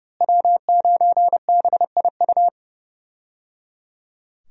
Morse code, heard as a keyed tone, about 30 wpm, 700 Hz. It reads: W9BIU
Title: USA Call Signs - 5 Characters - Repeated 3x - 30wpm